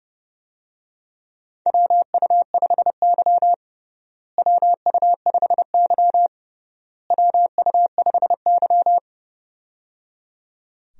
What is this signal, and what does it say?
Morse code, heard as a keyed tone, about 29 wpm, 700 Hz. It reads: WU5Y WU5Y WU5Y
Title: USA Call Signs - 4 Characters - Repeated 3x - 30wpm